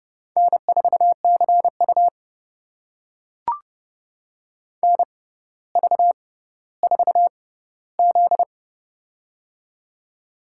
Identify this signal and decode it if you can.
Morse code, heard as a keyed tone, about 30 wpm, spaced as for 15 wpm, 700 Hz. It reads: N4CU NV4Z
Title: USA Call Signs - 4 Characters - 6x Character Spacing - 30wpm